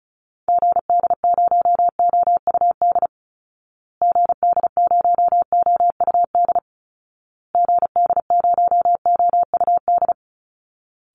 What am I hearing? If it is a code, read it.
Morse code, heard as a keyed tone, about 34 wpm, 700 Hz. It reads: GD0OUD GD0OUD GD0OUD
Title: Call Signs Repeated 2x - 35wpm